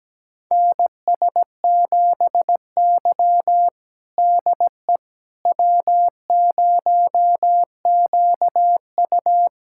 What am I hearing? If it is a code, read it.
Morse code, heard as a keyed tone, about 17 wpm, 700 Hz. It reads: NS7Y DE W0QU
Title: USA Callsign DE Callsign - 4 Characters - 17wpm